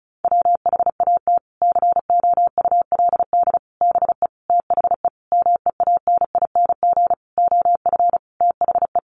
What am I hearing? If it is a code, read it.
Morse code, heard as a keyed tone, about 34 wpm, 700 Hz. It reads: WHAT COULD BE THE MEANING OF THE
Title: Sentences from Top 900 Words 35wpm